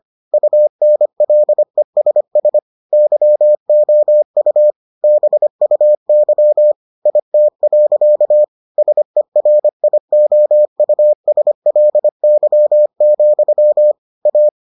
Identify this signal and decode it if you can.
Morse code, heard as a keyed tone, about 25 wpm, 600 Hz. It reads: UNLESS YOU BUY IT. SERIOUSLY, A